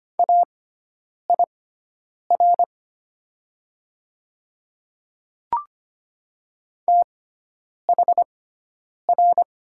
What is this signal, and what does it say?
Morse code, heard as a keyed tone, about 25 wpm, spaced as for 9 wpm, 700 Hz. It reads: AIR THR
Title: 6x Character Spacing - Top 200 Words - 25wpm